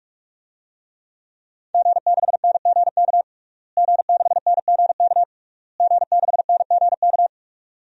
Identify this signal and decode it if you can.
Morse code, heard as a keyed tone, about 47 wpm, 700 Hz. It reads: G6NGK G6NGK G6NGK
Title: DXCC-100 Call Signs - 5 Characters - Repeated 3x - 45wpm